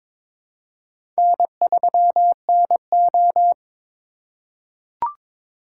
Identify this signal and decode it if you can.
Morse code, heard as a keyed tone, about 22 wpm, 700 Hz. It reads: N3NO
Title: USA Call Signs - 3x Character Spacing - 22wpm